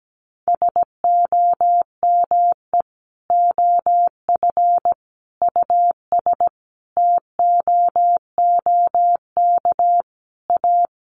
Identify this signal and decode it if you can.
Morse code, heard as a keyed tone, about 17 wpm, 700 Hz. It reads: SOME OF US TOOK A
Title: Sentences from Top 1400 Words 17wpm